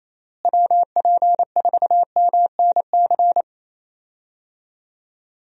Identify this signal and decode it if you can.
Morse code, heard as a keyed tone, about 27 wpm, 700 Hz. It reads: WP4MNC
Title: USA Call Signs - 6 Characters - Repeated 2x - 28wpm